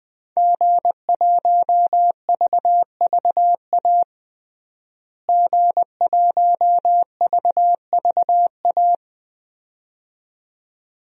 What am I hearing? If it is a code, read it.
Morse code, heard as a keyed tone, about 20 wpm, 700 Hz. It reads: G1VVA G1VVA
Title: DXCC-100 Call Signs - 5 Characters - Repeated 2x - 20wpm